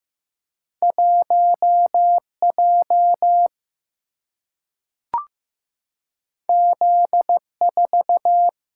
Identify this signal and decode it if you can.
Morse code, heard as a keyed tone, about 15 wpm, 700 Hz. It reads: 1J Z4
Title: Single-Letter Single-Number 15wpm